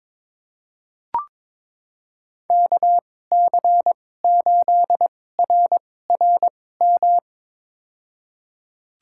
Morse code, heard as a keyed tone, about 22 wpm, spaced as for 16 wpm, 700 Hz. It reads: KC8RRM
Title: USA Call Signs - 2x Character Spacing - 22wpm